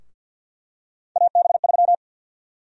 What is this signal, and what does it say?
Morse code, heard as a keyed tone, about 52 wpm, 700 Hz. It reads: AB3
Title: USA Call Sign Prefixes - 50wpm